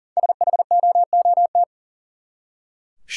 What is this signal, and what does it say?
Morse code, heard as a keyed tone, about 40 wpm, 700 Hz. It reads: SHOOT
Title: Five Letter Words - 40wpm